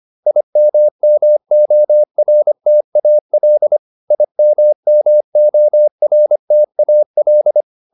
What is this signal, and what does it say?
Morse code, heard as a keyed tone, about 25 wpm, 600 Hz. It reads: IMMORTAL IMMORTAL